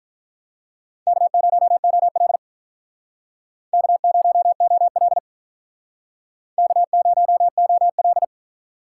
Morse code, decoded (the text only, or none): K0OL K0OL K0OL